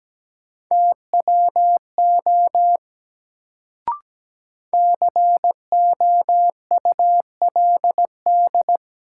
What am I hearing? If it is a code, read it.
Morse code, heard as a keyed tone, about 17 wpm, 700 Hz. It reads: TWO COULD
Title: Sets of 2 Words - New Word Drill - 17wpm